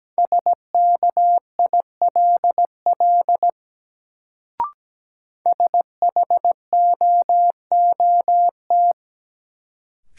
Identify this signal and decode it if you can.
Morse code, heard as a keyed tone, about 17 wpm, 700 Hz. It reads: SKILL SHOOT
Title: Sentences from Top 1200 Words - New Word Drill - 17wpm